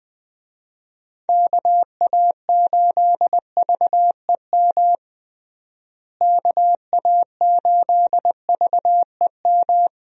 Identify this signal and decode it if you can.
Morse code, heard as a keyed tone, about 20 wpm, 700 Hz. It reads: KA8VEM KA8VEM
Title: USA Call Signs - 6 Characters - Repeated 2x - 20wpm